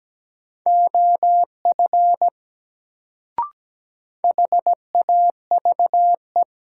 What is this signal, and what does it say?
Morse code, heard as a keyed tone, about 17 wpm, 700 Hz. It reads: OF HAVE
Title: Sentences from Top 100 Words - New Word Drill - 17wpm